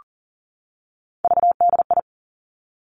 Morse code, heard as a keyed tone, about 38 wpm, 700 Hz. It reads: VDI
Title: Three Challenging Letter-Number Combinations 40wpm